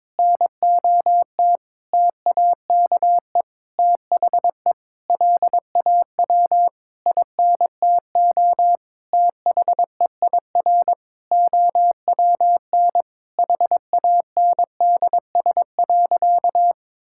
Morse code, heard as a keyed tone, about 22 wpm, 700 Hz. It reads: NOT TAKE THE LAW INTO THEIR OWN HANDS.
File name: practice / 22wpm / Sentences with 8-Word Phrases - 22wpm